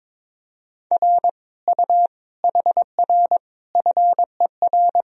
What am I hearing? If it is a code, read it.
Morse code, heard as a keyed tone, about 22 wpm, 700 Hz. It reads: R U HR FER
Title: Sentences from Top 200 Words and Abbreviations 22wpm